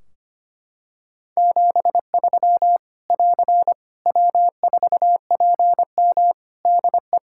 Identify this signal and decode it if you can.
Morse code, heard as a keyed tone, about 25 wpm, 700 Hz. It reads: 73 <AR> W4PM DE
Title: K1USN SST QSO Practice 25wpm